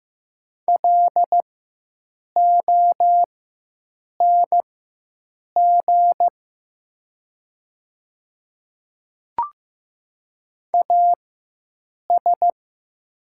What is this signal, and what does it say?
Morse code, heard as a keyed tone, about 15 wpm, spaced as for 7 wpm, 700 Hz. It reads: LONG AS
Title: 4x Character Spacing - Top 100 Words - 15wpm